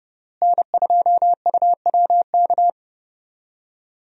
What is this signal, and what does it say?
Morse code, heard as a keyed tone, about 30 wpm, 700 Hz. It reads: N2UWK
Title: USA Call Signs - 5 Characters - Repeated 3x - 30wpm